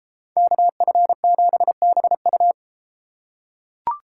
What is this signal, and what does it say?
Morse code, heard as a keyed tone, about 33 wpm, 700 Hz. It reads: KF7BU